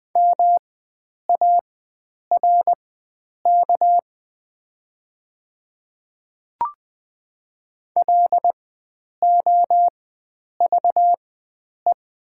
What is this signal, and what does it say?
Morse code, heard as a keyed tone, about 20 wpm, spaced as for 9 wpm, 700 Hz. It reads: MARK LOVE
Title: 4x Character Spacing - Top 500 Words - 20wpm